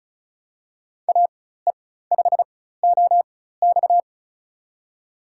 Morse code, heard as a keyed tone, about 36 wpm, spaced as for 16 wpm, 700 Hz. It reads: AE5OX
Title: USA Call Signs - 5 Characters - 4x Character Spacing - 35wpm